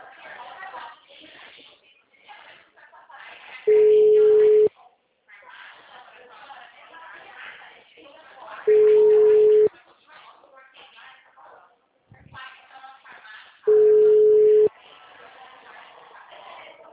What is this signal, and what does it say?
Ringback tone (Europe, 425 Hz), with speech.
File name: out- -2010- - - .32518